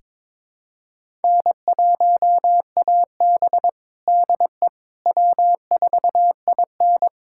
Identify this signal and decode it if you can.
Morse code, heard as a keyed tone, about 21 wpm, 700 Hz. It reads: N1AB DE W4IN
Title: USA Callsign DE Callsign - 4 Characters - 22wpm